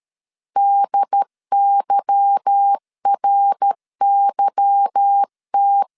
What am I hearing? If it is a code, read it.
Morse code, heard as a keyed tone, about 13 wpm, 800 Hz. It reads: DYRYT
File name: LEZIONI / 8 YJ45 / 8 Y 12wpm 2228